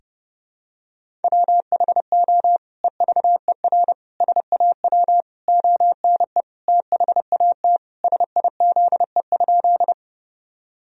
Morse code, heard as a keyed tone, about 32 wpm, 700 Hz. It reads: WHO EVER SAW ONE THAT SIZE?
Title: Sentences from Top 1000 Words - Review Part 2 - 30wpm